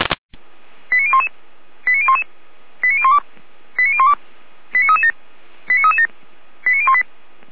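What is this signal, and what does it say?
Five-tone selective call (ZVEI), heard as quick runs of five tones, tones of about 70 ms. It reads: 89011 89011 89012 89012 89337 89337 89018
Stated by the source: ZVEI five-tone selective call received by SDR.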